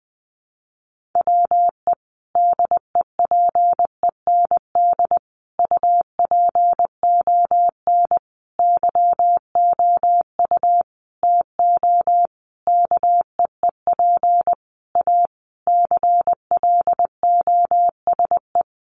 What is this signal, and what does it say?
Morse code, heard as a keyed tone, about 20 wpm, 700 Hz. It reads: WE DEPEND UPON YOU TO KEEP A CLOSE
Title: Sentences from Top 1300 Words 20wpm